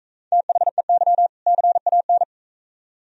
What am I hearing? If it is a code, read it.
Morse code, heard as a keyed tone, about 44 wpm, 700 Hz. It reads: THEY CAN